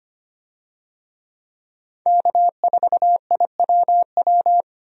Morse code, heard as a keyed tone, about 25 wpm, 700 Hz. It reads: K4IWW
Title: USA Call Signs - 5 Characters - 25wpm